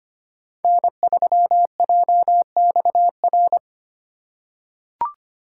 Morse code, heard as a keyed tone, about 25 wpm, 700 Hz. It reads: N3JXR
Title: USA Call Signs - 5 Characters - 2x Character Spacing - 25wpm